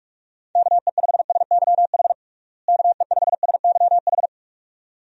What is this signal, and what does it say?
Morse code, heard as a keyed tone, about 41 wpm, 700 Hz. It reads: KE5SYH KE5SYH
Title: USA Call Signs - 6 Characters - Repeated 2x - 45wpm